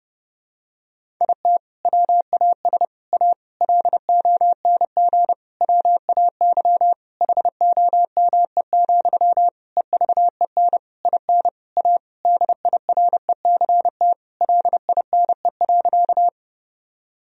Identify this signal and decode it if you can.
Morse code, heard as a keyed tone, about 30 wpm, 700 Hz. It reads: IT WAS A LONG WAY HOME, EVEN IN A DIRECT LINE.
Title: Sentences from Top 1000 Words - Review Part 2 - 30wpm